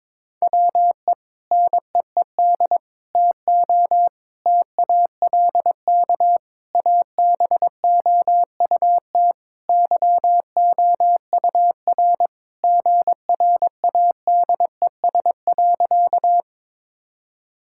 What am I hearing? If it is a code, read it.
Morse code, heard as a keyed tone, about 22 wpm, 700 Hz. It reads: WE NEED TO TALK ABOUT YOUR GRADES.
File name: practice / 22wpm / Sentences with 3-Word Phrases - 22wpm